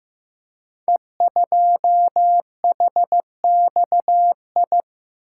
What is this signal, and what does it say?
Morse code, heard as a keyed tone, about 15 wpm, 700 Hz. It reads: E2HXI
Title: Code Groups - 5 Letter:Numbers - 15wpm